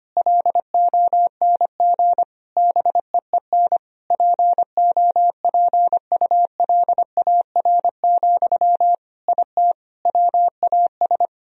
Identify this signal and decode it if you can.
Morse code, heard as a keyed tone, about 25 wpm, 700 Hz. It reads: LONG BEEN POPULAR, IT WAS
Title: Sentences from Top 1000 Words - Review Part 3 - 25wpm